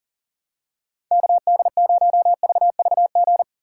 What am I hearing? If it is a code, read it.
Morse code, heard as a keyed tone, about 40 wpm, 700 Hz. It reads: KD0VVG
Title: USA Call Signs - 4x Character Spacing - 40wpm